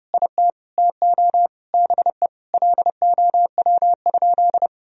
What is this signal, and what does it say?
Morse code, heard as a keyed tone, about 29 wpm, 700 Hz. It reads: IT TO BE LOW?
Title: Sentences from Top 1000 Words - Review Part 1 - 30wpm